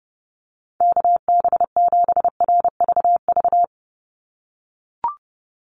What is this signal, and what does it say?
Morse code, heard as a keyed tone, about 30 wpm, 700 Hz. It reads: KB7RVV
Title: USA Call Signs 30wpm